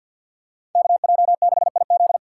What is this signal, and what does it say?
Morse code, heard as a keyed tone, about 52 wpm, 700 Hz. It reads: KJ6IZ
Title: USA Call Signs - 5 Characters - 3x Character Spacing - 50wpm